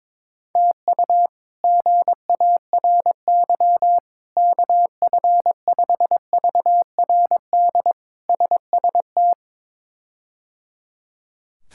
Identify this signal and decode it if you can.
Morse code, heard as a keyed tone, about 22 wpm, 700 Hz. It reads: TU GARY KF5VRD SST
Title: K1USN SST QSO Practice - 22wpm